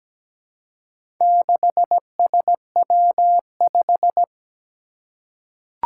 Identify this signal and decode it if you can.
Morse code, heard as a keyed tone, about 17 wpm, 700 Hz. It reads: 6SW5
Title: Four Challenging Letter-Number Combinations 17wpm